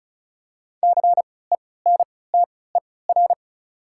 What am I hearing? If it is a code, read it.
Morse code, heard as a keyed tone, about 36 wpm, spaced as for 20 wpm, 690 Hz. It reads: CENTER
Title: 3x Character Spacing - Top 500 Words - 35wpm